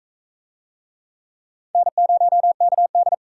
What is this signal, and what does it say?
Morse code, heard as a keyed tone, about 44 wpm, 700 Hz. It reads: N0KD